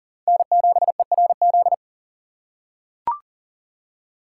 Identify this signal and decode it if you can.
Morse code, heard as a keyed tone, about 40 wpm, 700 Hz. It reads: N7ERZ
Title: USA Call Signs - 5 Characters - 3x Character Spacing - 40wpm